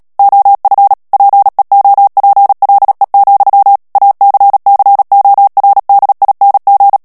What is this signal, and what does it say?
Morse code, heard as a keyed tone, about 37 wpm, 800 Hz. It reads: OF PEOPLE, ACCORDING
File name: morse mp3 / ANGLAIS texte clair 2 vitesse 35 mots minute